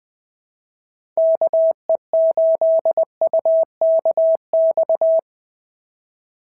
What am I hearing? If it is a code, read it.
Morse code, heard as a keyed tone, about 19 wpm, 650 Hz. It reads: KE8UKX
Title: POTA QSO Practice - 20wpm